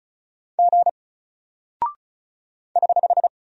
Morse code, heard as a keyed tone, about 36 wpm, 700 Hz. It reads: G <HH>
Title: Single Character (Letter:Number:Punctuation:Prosign) Alt 35wpm